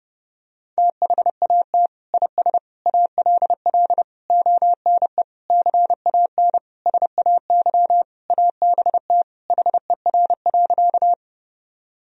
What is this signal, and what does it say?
Morse code, heard as a keyed tone, about 32 wpm, 700 Hz. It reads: THAT IS ALL ONE CAN SAY ABT HER.
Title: Sentences from Top 200 Words and Abbreviations 30wpm